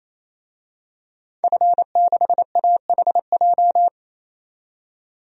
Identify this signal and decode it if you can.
Morse code, heard as a keyed tone, about 28 wpm, 700 Hz. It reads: F6AHJ
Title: DXCC-100 Call Signs - 5 Characters - 28wpm